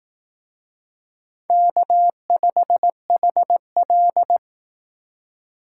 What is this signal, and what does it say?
Morse code, heard as a keyed tone, about 18 wpm, 700 Hz. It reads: K5HL
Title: Speed Racing - USA 1x2 Call Signs 15wpm